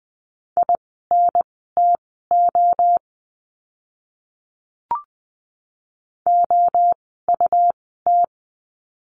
Morse code, heard as a keyed tone, about 20 wpm, spaced as for 14 wpm, 700 Hz. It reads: INTO OUT
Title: 192 Lesson 31f - Letter v - Words 20wpm